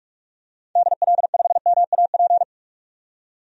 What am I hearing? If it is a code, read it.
Morse code, heard as a keyed tone, about 46 wpm, 690 Hz. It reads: DL5MAP
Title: DXCC-100 Call Signs - 6 Characters - 45wpm